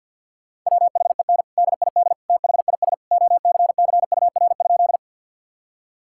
Morse code, heard as a keyed tone, about 50 wpm, 700 Hz. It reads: WHEN DID THIS OCCUR?
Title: Sentences from Top 1000 Words - Review Part 6 - 50wpm